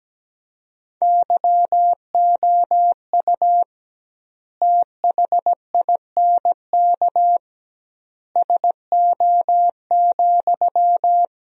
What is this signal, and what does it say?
Morse code, heard as a keyed tone, about 17 wpm, 700 Hz. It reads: YOU THINK SO,
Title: 2x Word-Spacing - Sentences from Top 200 Words - 17wpm